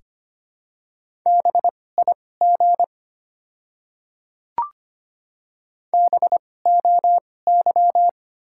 Morse code, heard as a keyed tone, about 25 wpm, spaced as for 18 wpm, 700 Hz. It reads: BIG BOY
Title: 2x Character Spacing - Top 200 Words - 25wpm